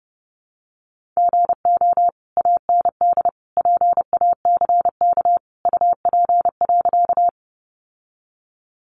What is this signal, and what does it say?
Morse code, heard as a keyed tone, about 30 wpm, 700 Hz. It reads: GO AND PACK UP.
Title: Sentences from Top 1300 Words 30wpm